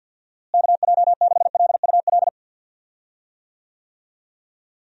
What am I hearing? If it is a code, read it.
Morse code, heard as a keyed tone, about 51 wpm, 700 Hz. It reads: KJ6LUL
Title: USA Call Signs - 6 Characters - Repeated 2x - 50wpm